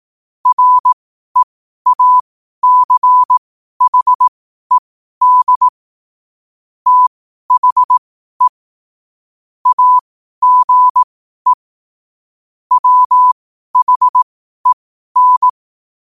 Morse code, heard as a keyed tone, about 18 wpm, spaced as for 12 wpm, 1000 Hz. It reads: REACHED THE AGE WHEN
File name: qotdfiles / 12wpm